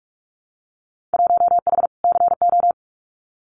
Morse code, heard as a keyed tone, about 46 wpm, 700 Hz. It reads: 1H CO
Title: Winter Field Day Exchanges 45wpm